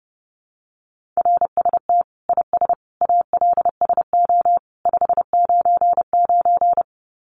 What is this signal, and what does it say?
Morse code, heard as a keyed tone, about 30 wpm, 700 Hz. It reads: RST IS ALSO 599
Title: Common QSO Phrases 30wpm